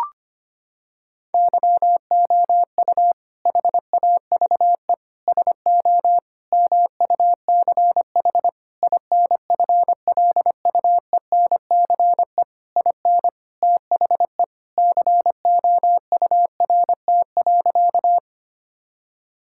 Morse code, heard as a keyed tone, about 25 wpm, 700 Hz. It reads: YOU HAVE SO MUCH INFLUENCE IN THE COURT.
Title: Sentences from Top 1600 Words 25wpm